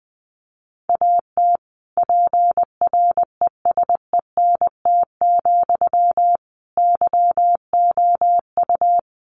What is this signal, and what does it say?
Morse code, heard as a keyed tone, about 20 wpm, 700 Hz. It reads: AT PRESENT, YOU
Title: Sentences from Top 800 Words 20wpm